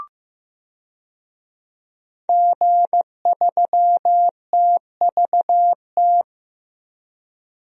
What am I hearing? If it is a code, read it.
Morse code, heard as a keyed tone, about 15 wpm, 700 Hz. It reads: G3TVT
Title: DXCC-100 Call Signs - 5 Characters - 15wpm